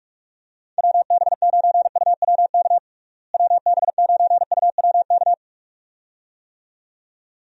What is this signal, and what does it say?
Morse code, heard as a keyed tone, about 46 wpm, 700 Hz. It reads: WB9UWK WB9UWK
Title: USA Call Signs - 6 Characters - Repeated 2x - 45wpm